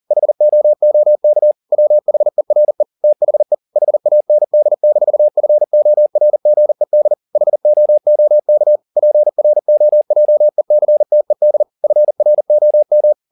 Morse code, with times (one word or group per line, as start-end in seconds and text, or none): 0.1-1.5 HOOK
1.7-2.8 WHERE
3.0-3.5 THE
3.8-7.1 HAND-FORGED
7.4-8.8 HOOK
9.0-11.6 PROJECTED
11.8-13.1 FROM